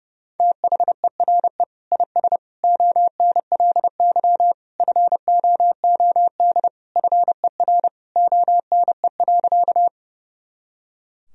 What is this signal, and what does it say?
Morse code, heard as a keyed tone, about 30 wpm, 700 Hz. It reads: THERE IS ONLY FOOD FER ONE.
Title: Sentences from Top 300 Words and Abbreviations 30wpm